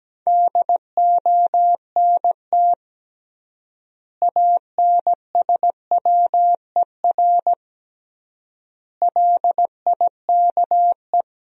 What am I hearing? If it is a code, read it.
Morse code, heard as a keyed tone, about 17 wpm, 700 Hz. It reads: DONT ANSWER LIKE
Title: 3x Word-Spacing - Sentences from Top 400 Words - 17wpm